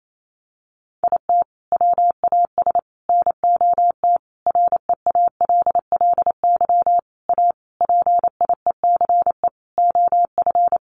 Morse code, heard as a keyed tone, about 28 wpm, 700 Hz. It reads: IT WAS NOT REALLY A PIECE OF
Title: Sentences from Top 600 Words 28wpm